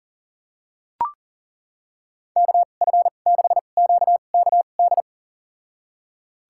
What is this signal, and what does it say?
Morse code, heard as a keyed tone, about 40 wpm, spaced as for 29 wpm, 700 Hz. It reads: KF6QKD
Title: USA Call Signs - 6 Characters - 2x Character Spacing - 40wpm